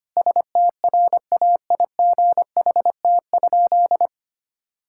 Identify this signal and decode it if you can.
Morse code, heard as a keyed tone, about 25 wpm, 700 Hz. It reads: STRAIGHT?
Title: Sentences from Top 1000 Words - Review Part 6 - 25wpm